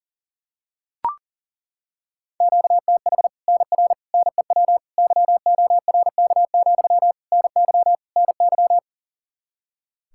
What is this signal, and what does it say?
Morse code, heard as a keyed tone, about 40 wpm, 700 Hz. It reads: QTH NR NEW YORK, NY NY
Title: Practice QSO - Exchange 1 - Beginning - 40wpm